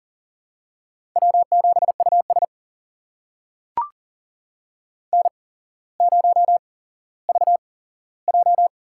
Morse code, heard as a keyed tone, about 40 wpm, spaced as for 17 wpm, 700 Hz. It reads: W7US N0VJ
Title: USA Call Signs - 4 Characters - 8x Character Spacing - 40wpm